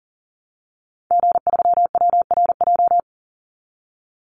Morse code, heard as a keyed tone, about 40 wpm, 700 Hz. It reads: G3WRJ
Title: Call Signs Repeated 3x - 40wpm